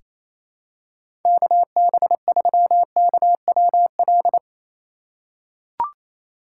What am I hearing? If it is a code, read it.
Morse code, heard as a keyed tone, about 28 wpm, 700 Hz. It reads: KB3KWL
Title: USA Call Signs - 6 Characters - 2x Character Spacing - 28wpm